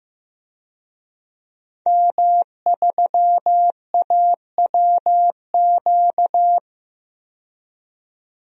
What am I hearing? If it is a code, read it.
Morse code, heard as a keyed tone, about 15 wpm, 700 Hz. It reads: M3AWQ
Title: DXCC-100 Call Signs - 5 Characters - 15wpm